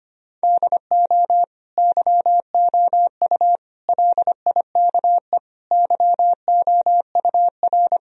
Morse code, heard as a keyed tone, about 25 wpm, 700 Hz. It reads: DO YOU LIKE YOUR
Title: Sentences from Top 1000 Words - Review Part 4 - 25wpm